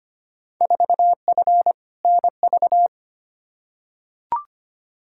Morse code, heard as a keyed tone, about 25 wpm, 700 Hz. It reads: 4F NV
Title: Field Day Exchanges Encore - 25wpm